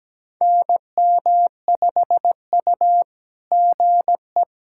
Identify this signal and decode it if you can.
Morse code, heard as a keyed tone, about 17 wpm, 700 Hz. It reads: NM5U GE
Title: SOTA QSO Practice 17wpm